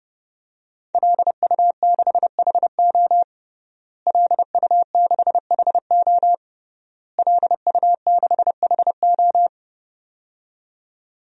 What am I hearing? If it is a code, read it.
Morse code, heard as a keyed tone, about 30 wpm, 700 Hz. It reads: LU6HO LU6HO LU6HO
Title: DXCC-100 Call Signs - 5 Characters - Repeated 3x - 30wpm